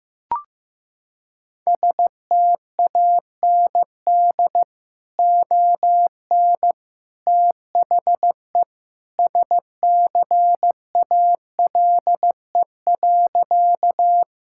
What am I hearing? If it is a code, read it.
Morse code, heard as a keyed tone, about 15 wpm, 700 Hz. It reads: STAND ON THE SCALE.
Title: Sentences from Top 1000 Words - Review Part 5 - 15wpm